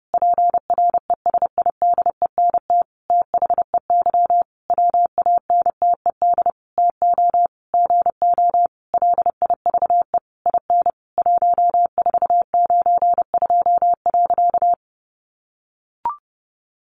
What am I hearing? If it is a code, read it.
Morse code, heard as a keyed tone, about 30 wpm, 700 Hz. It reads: PRESIDENT THEY WANTED TO GO LIVE IN 1492.
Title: A Sound Of Thunder 30wpm